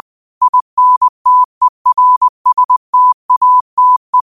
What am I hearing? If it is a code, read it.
Morse code, heard as a keyed tone, about 20 wpm, 1000 Hz. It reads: INTERSTATE
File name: qotdfiles / 20wpm